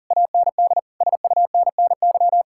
Morse code, heard as a keyed tone, about 52 wpm, 700 Hz. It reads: AND SUNNY